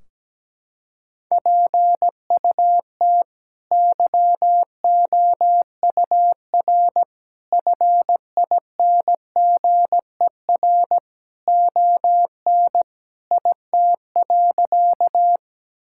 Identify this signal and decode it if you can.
Morse code, heard as a keyed tone, about 17 wpm, 700 Hz. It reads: PUT YOUR FINGER ON IT.
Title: Sentences from Top 1600 Words 17wpm